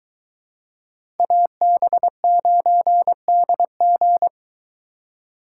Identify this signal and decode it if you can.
Morse code, heard as a keyed tone, about 23 wpm, 700 Hz. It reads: AB9DG